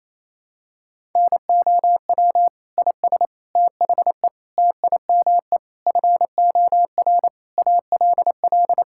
Morse code, heard as a keyed tone, about 28 wpm, 700 Hz. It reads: NOW IS THE TIME FOR ALL
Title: Pangram Challenge v2 - 28wpm